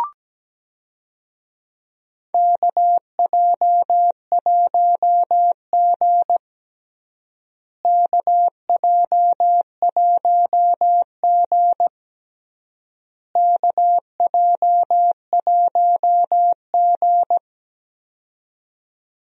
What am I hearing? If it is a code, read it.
Morse code, heard as a keyed tone, about 17 wpm, 700 Hz. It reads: KJ1G KJ1G KJ1G
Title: USA Call Signs - 4 Characters - Repeated 3x - 17wpm